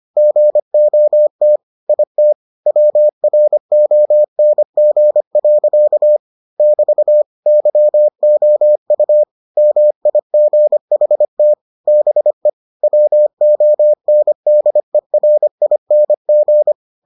Morse code, heard as a keyed tone, about 25 wpm, 600 Hz. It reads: GOT IT WRONG. <BT> YOU MIGHT BE WONDERING